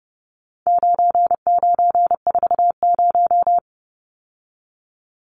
Morse code, heard as a keyed tone, about 30 wpm, 700 Hz. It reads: 9940
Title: Numbers - 4-Digit 30wpm